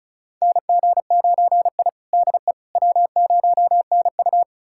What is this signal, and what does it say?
Morse code, heard as a keyed tone, about 36 wpm, 700 Hz. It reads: NG9I DE W0NU
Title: USA Callsign DE Callsign - 4 Characters - 35wpm